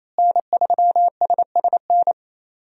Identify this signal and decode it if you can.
Morse code, heard as a keyed tone, about 28 wpm, 700 Hz. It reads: N3SSN
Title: USA Call Signs - 5 Characters - Repeated 3x - 28wpm